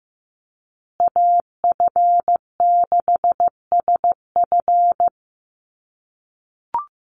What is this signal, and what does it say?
Morse code, heard as a keyed tone, about 15 wpm, 700 Hz. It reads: AF6SF
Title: USA Call Signs Repeated 3x 15wpm